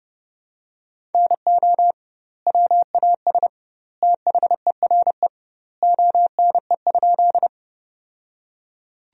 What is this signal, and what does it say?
Morse code, heard as a keyed tone, about 30 wpm, 700 Hz. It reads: NO WAS THERE ONE?
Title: 2x Word-Spacing - Sentences from Top 100 Words - 30wpm